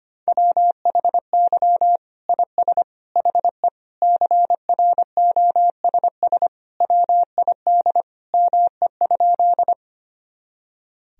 Morse code, heard as a keyed tone, about 25 wpm, 700 Hz. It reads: WHY IS HE CROSS WID ME?
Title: Sentences from Top 300 Words and Abbreviations 25wpm